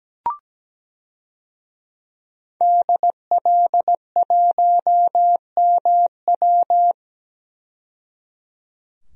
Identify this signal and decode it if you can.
Morse code, heard as a keyed tone, about 17 wpm, 700 Hz. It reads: DL1MW
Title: DXCC-100 Call Signs - 5 Characters - 17wpm